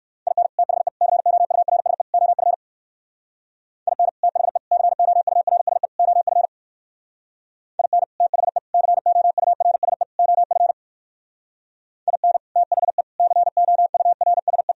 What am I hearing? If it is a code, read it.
Morse code, heard as a keyed tone, about 64 wpm, 700 Hz. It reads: IN THE COURSE OF IN THE COURSE OF IN THE COURSE OF IN THE COURSE